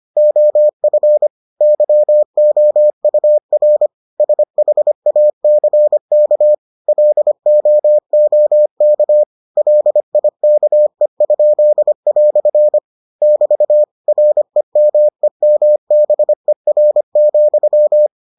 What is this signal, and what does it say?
Morse code, heard as a keyed tone, about 25 wpm, 600 Hz. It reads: OF YOUR SHACK LOOK LIKE?" <BT> REMEMBER,